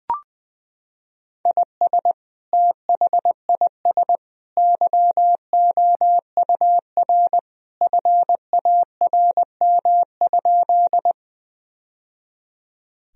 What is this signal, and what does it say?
Morse code, heard as a keyed tone, about 20 wpm, 700 Hz. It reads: IS THIS YOUR FARM?
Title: Sentences from Top 1000 Words - Review Part 2 - 20wpm